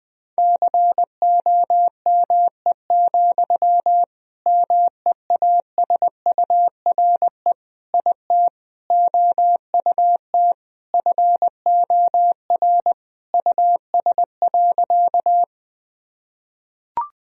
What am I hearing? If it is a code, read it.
Morse code, heard as a keyed tone, about 20 wpm, 700 Hz. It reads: COME, MEASURE IT OUT FOR US.
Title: Sentences from Top 1000 Words - Review Part 2 - 20wpm